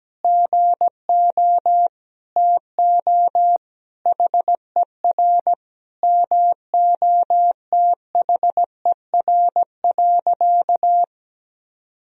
Morse code, heard as a keyed tone, about 17 wpm, 700 Hz. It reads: GO TO HER MOTHER.
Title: Sentences from Top 1000 Words - Review Part 4 - 17wpm